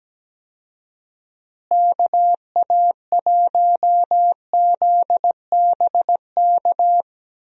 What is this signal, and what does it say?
Morse code, heard as a keyed tone, about 17 wpm, 700 Hz. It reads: KA1ZBK
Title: USA Call Signs - 6 Characters - Repeated 3x - 17wpm